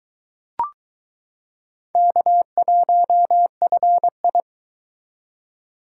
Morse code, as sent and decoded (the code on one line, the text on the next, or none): -.- .---- ..-. ..
K1FI